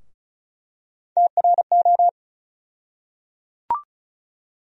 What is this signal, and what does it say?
Morse code, heard as a keyed tone, about 34 wpm, 700 Hz. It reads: TRO
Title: USA Call Sign Postfixes - 3 Letters 35wpm